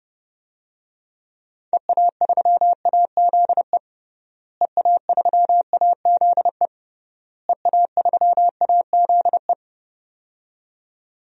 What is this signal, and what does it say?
Morse code, heard as a keyed tone, about 32 wpm, 700 Hz. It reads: EA3AZE EA3AZE EA3AZE
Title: DXCC-100 Call Signs - 6 Characters - Repeated 3x - 30wpm